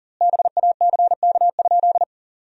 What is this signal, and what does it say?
Morse code, heard as a keyed tone, about 40 wpm, 700 Hz. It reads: BACK?
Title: 2x Word-Spacing - Sentences from Top 300 Words - 40wpm